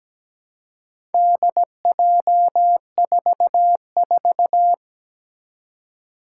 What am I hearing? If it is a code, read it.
Morse code, heard as a keyed tone, about 17 wpm, 700 Hz. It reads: DJ44
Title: Four Challenging Letter-Number Combinations 17wpm